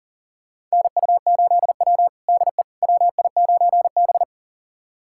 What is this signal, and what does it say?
Morse code, heard as a keyed tone, about 40 wpm, 700 Hz. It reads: NU8W DE WI9B
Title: USA Callsign DE Callsign - 4 Characters - 40wpm